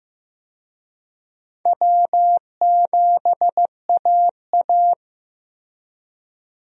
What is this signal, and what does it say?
Morse code, heard as a keyed tone, about 15 wpm, 700 Hz. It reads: W7AA
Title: USA Call Signs - 4 Characters - 15wpm